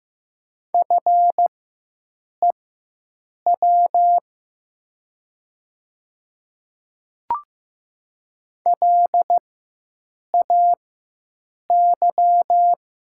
Morse code, heard as a keyed tone, about 15 wpm, spaced as for 7 wpm, 700 Hz. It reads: FEW LAY
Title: 4x Character Spacing - Top 500 Words - 15wpm